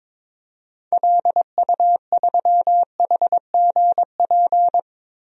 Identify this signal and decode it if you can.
Morse code, heard as a keyed tone, about 21 wpm, 700 Hz. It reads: LU3HGP
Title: DXCC-100 Call Signs - 6 Characters - 22wpm